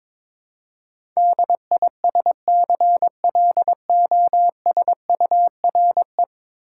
Morse code, heard as a keyed tone, about 23 wpm, 700 Hz. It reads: DISCLOSURE
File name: practice / 22wpm / 10 Letter Words - First 3 Letters Given - 22wpm